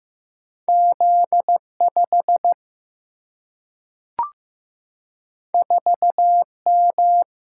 Morse code, heard as a keyed tone, about 15 wpm, 700 Hz. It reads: Z 5 4 M
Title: Single-Letter Single-Number 15wpm